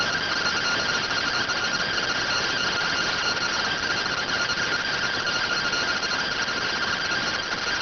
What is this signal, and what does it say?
Two-tone teleprinter signal, about 170 Hz shift.